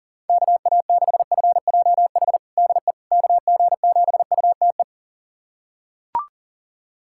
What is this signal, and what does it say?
Morse code, heard as a keyed tone, about 37 wpm, 690 Hz. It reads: KA6FJH DE KG7UTE
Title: USA Callsign DE Callsign - 6 Characters - 40wpm